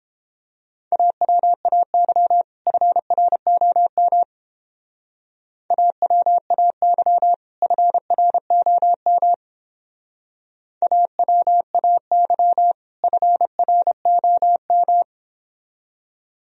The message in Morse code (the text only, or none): AWAY FROM AWAY FROM AWAY FROM